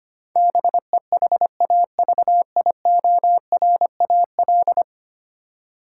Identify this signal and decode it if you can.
Morse code, heard as a keyed tone, about 25 wpm, 700 Hz. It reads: BEHAVIORAL
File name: 10 Letter Words - First 3 Letters Given - 25wpm